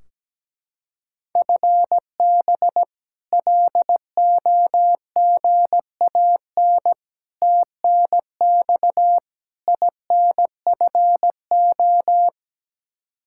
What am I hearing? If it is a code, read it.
Morse code, heard as a keyed tone, about 17 wpm, 700 Hz. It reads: FB LOGAN TNX INFO
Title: K1USN SST QSO Practice 17wpm